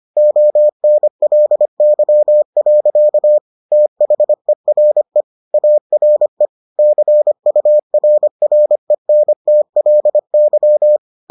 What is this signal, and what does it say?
Morse code, heard as a keyed tone, about 25 wpm, 600 Hz. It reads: ONLY. THERE ARE CURRENTLY